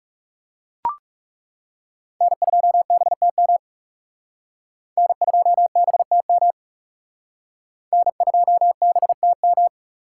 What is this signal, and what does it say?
Morse code, heard as a keyed tone, about 39 wpm, 700 Hz. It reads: N2BTM N2BTM N2BTM